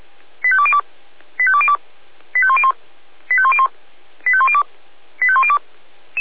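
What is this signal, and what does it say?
Five-tone selective call (ZVEI), heard as quick runs of five tones, tones of about 70 ms. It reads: 85202 85202 85101 85101 85102 85102
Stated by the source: ZVEI five-tone selective call, code 85814, received via SDR.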